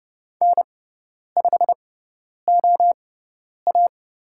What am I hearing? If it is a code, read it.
Morse code, heard as a keyed tone, about 30 wpm, spaced as for 10 wpm, 700 Hz. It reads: N5OA
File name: Lesson 16g - Letter c - Call Signs 30wpm